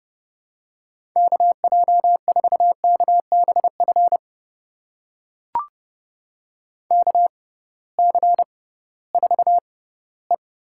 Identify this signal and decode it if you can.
Morse code, heard as a keyed tone, about 30 wpm, spaced as for 18 wpm, 700 Hz. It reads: KJ4KBF KC4E
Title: USA Call Signs - 6 Characters - 6x Character Spacing - 30wpm